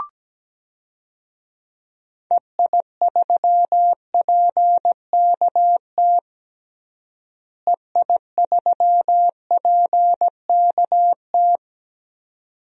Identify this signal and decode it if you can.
Morse code, heard as a keyed tone, about 17 wpm, 700 Hz. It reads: EI3PKT EI3PKT
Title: DXCC-100 Call Signs - 6 Characters - Repeated 2x - 17wpm